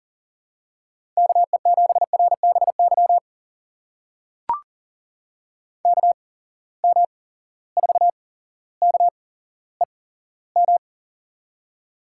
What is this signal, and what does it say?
Morse code, heard as a keyed tone, about 40 wpm, spaced as for 17 wpm, 700 Hz. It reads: KE7RBY KM4KEM
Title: USA Call Signs - 6 Characters - 8x Character Spacing - 40wpm